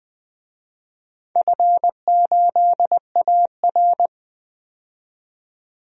Morse code, heard as a keyed tone, about 20 wpm, 700 Hz. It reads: F8AR
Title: DXCC-100 Call Signs - 4 Characters - 20wpm